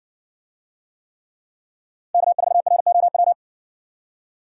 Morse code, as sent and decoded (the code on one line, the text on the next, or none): -.- ....- .-. --- .--
K4ROW